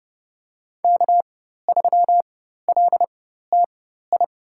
Morse code, heard as a keyed tone, about 30 wpm, spaced as for 14 wpm, 700 Hz. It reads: K3LTI
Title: USA Call Signs - 5 Characters - 4x Character Spacing - 30wpm